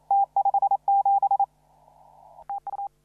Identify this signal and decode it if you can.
Morse code, heard as a keyed tone, about 33 wpm, 800 Hz. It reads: T57 TU
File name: CW / RHOB CW 2017 HA6P 0001